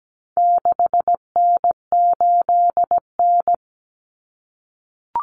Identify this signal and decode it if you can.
Morse code, heard as a keyed tone, about 17 wpm, 700 Hz. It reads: 6N8N
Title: Common Cut Numbers - 4-Digit 17wpm